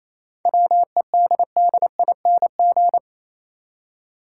Morse code, heard as a keyed tone, about 28 wpm, 700 Hz. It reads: WEDDING
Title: Seven Letter Words - 28wpm